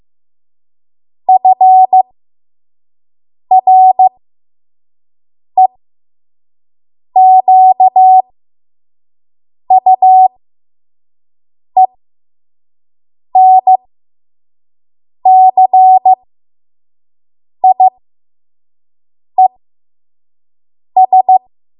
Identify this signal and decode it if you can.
Morse code, heard as a keyed tone, about 15 wpm, spaced as for 5 wpm, 750 Hz. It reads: FREQUENCIES
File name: Morse / 05WPM